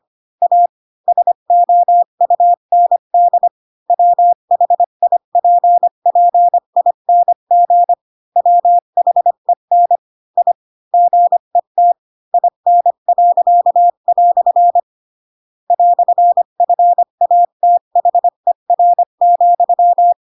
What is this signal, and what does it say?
Morse code, heard as a keyed tone, about 25 wpm, 700 Hz. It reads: A SOUND WHIPPING WHEN I GET IN." "FATHER,
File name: Pinocchio / Pinocchio ch7 20 25 ik1hge